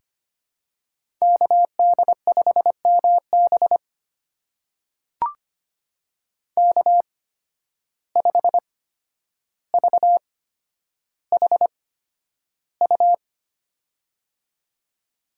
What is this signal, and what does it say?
Morse code, heard as a keyed tone, about 25 wpm, spaced as for 11 wpm, 700 Hz. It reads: KD5MB K5VHU
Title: USA Call Signs - 5 Characters - 8x Character Spacing - 25wpm